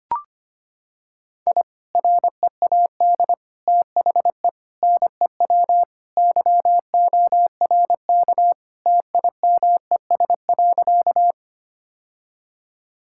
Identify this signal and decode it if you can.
Morse code, heard as a keyed tone, about 25 wpm, 700 Hz. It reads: I READ THE NEW YORK TIMES.
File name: practice / 25wpm / Sentences with 3-Word Phrases - 25wpm